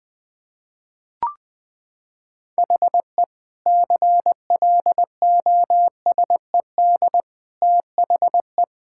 Morse code, heard as a keyed tone, about 20 wpm, 700 Hz. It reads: HE CLOSED THE
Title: Sentences from Top 1600 Words 20wpm